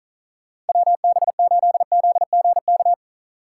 Morse code, heard as a keyed tone, about 44 wpm, 700 Hz. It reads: WB8ZGK